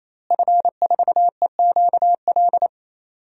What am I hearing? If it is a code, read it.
Morse code, heard as a keyed tone, about 28 wpm, 700 Hz. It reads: F4EQL
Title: DXCC-100 Call Signs - 5 Characters - Repeated 3x - 28wpm